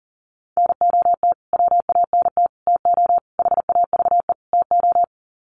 Morse code, heard as a keyed tone, about 40 wpm, 700 Hz. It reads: NOT WANT TO HAVE TO
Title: Sets of 5 Words Encore2 40wpm